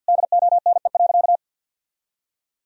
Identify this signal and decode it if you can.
Morse code, heard as a keyed tone, about 51 wpm, 700 Hz. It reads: DONE.
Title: Sentences from Top 1000 Words - Review Part 1 - 50wpm